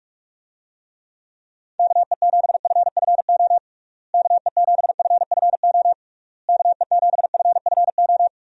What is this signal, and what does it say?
Morse code, heard as a keyed tone, about 46 wpm, 700 Hz. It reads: KE7FFO KE7FFO KE7FFO
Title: USA Call Signs - 6 Characters - Repeated 3x - 45wpm